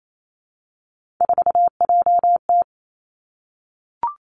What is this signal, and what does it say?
Morse code, heard as a keyed tone, about 28 wpm, 700 Hz. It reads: 4JT
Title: Three Challenging Letter-Number Combinations 28wpm